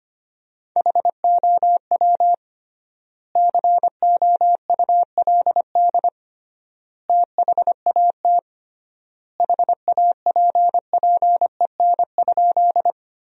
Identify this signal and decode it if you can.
Morse code, heard as a keyed tone, about 25 wpm, 700 Hz. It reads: HOW COULD THAT HAPPEN?
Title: 3x Word-Spacing - Sentences from Top 400 Words - 25wpm